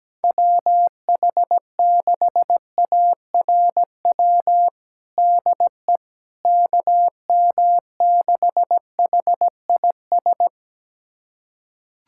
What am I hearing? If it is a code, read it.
Morse code, heard as a keyed tone, about 17 wpm, 700 Hz. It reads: WH6ARW DE KM6HIS
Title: USA Callsign DE Callsign - 6 Characters - 17wpm